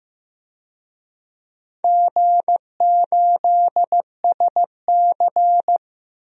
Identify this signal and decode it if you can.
Morse code, heard as a keyed tone, about 15 wpm, 700 Hz. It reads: G8SC
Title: DXCC-100 Call Signs - 4 Characters - Repeated 3x - 15wpm